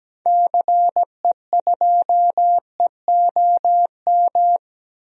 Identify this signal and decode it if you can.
Morse code, heard as a keyed tone, about 17 wpm, 700 Hz. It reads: CE2EOM
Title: DXCC-100 Call Signs - 6 Characters - 17wpm